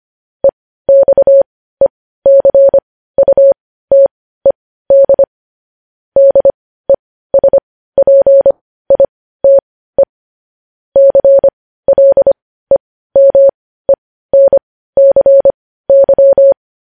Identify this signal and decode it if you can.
Morse code, heard as a keyed tone, about 25 wpm, spaced as for 15 wpm, 550 Hz. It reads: EXECUTED DESPITE CLEMENCY